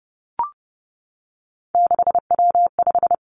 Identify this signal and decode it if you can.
Morse code, heard as a keyed tone, about 30 wpm, 700 Hz. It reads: 6W5
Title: Three Challenging Letter-Number Combinations 30wpm